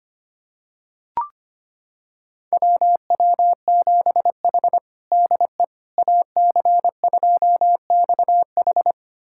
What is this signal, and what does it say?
Morse code, heard as a keyed tone, about 25 wpm, 700 Hz. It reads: WW7H DE AC2XH
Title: USA Callsign DE Callsign 25wpm